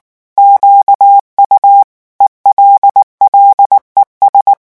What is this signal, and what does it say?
Morse code, heard as a keyed tone, about 19 wpm, 800 Hz. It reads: QU ELLES
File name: morse mp3 / revision9 citations humoristiques vitesse 18 mots minute